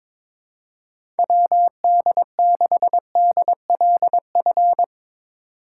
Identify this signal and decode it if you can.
Morse code, heard as a keyed tone, about 21 wpm, 700 Hz. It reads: WD6DLF
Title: USA Call Signs - 6x Character Spacing - 22wpm